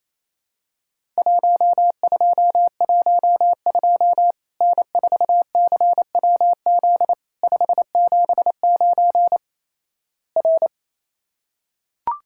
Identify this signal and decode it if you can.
Morse code, heard as a keyed tone, about 28 wpm, 700 Hz. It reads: 1212 N4CWZ 579 R
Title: Worked All Europe DX Contest - QTC Exchange - 28wpm